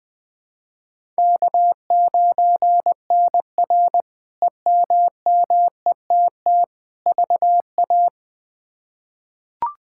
Morse code, heard as a keyed tone, about 20 wpm, 700 Hz. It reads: K9NR EMMETT VA
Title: K1USN SST QSO Practice - 20wpm